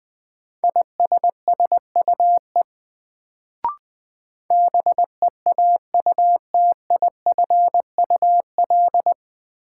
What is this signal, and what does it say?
Morse code, heard as a keyed tone, about 20 wpm, 700 Hz. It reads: ISSUE BEAUTIFUL
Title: Sentences from Top 900 Words - New Word Drill - 20wpm